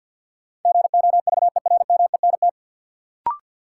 Morse code, heard as a keyed tone, about 47 wpm, 700 Hz. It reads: GOVERMENT
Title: Sentences from Top 1300 Words - New Word Drill - 50wpm